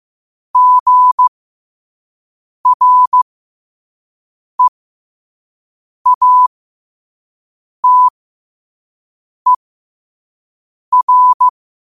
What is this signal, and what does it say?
Morse code, heard as a keyed tone, about 15 wpm, spaced as for 5 wpm, 1000 Hz. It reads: GREATER
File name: qotdfiles / 5wpm